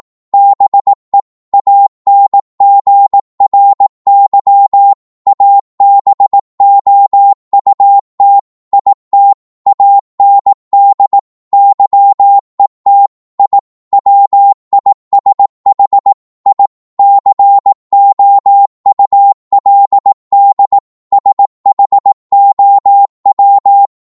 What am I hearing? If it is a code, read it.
Morse code, heard as a keyed tone, about 18 wpm, 800 Hz. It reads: BE ANGRY ABOUT IT AND YET I WISH I COULD SHOW
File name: Morse / psg-18wpm